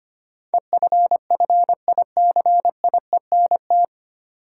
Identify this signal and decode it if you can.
Morse code, heard as a keyed tone, about 25 wpm, 700 Hz. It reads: EFFICIENT
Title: Sentences from Top 1300 Words - New Word Drill - 25wpm